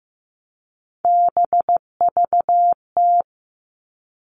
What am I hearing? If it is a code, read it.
Morse code, heard as a keyed tone, about 15 wpm, 700 Hz. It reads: BVT
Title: All Cut Numbers - 3-Digit 15wpm